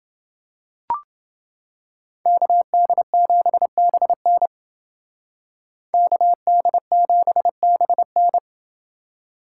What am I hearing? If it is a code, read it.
Morse code, heard as a keyed tone, about 28 wpm, 700 Hz. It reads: KD7BN KD7BN